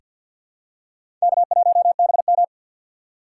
Morse code, heard as a keyed tone, about 49 wpm, 700 Hz. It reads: K1BM
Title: USA Call Signs - 3x Character Spacing - 50wpm